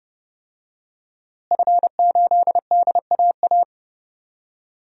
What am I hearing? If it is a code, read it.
Morse code, heard as a keyed tone, about 30 wpm, 700 Hz. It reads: F8DAA
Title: DXCC-100 Call Signs - 5 Characters - Repeated 3x - 30wpm